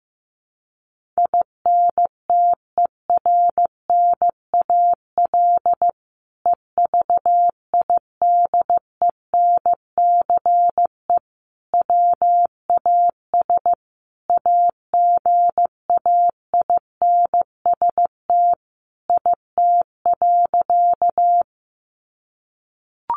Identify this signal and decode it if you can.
Morse code, heard as a keyed tone, about 15 wpm, 700 Hz. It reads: INTERNAL EVIDENCE WAS AGAINST IT.
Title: Sentences from Top 1400 Words 15wpm